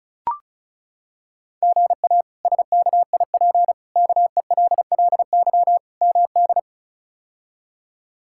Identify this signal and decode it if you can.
Morse code, heard as a keyed tone, about 34 wpm, 700 Hz. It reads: GA SKIP KELLY MD
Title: K1USN SST QSO Practice - 35wpm